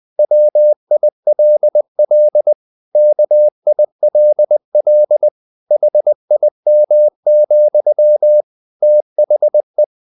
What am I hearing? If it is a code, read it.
Morse code, heard as a keyed tone, about 20 wpm, 600 Hz. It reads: WILL KILL HIM, THE
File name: the old man and the sea / the old man and the sea part 1 20wpm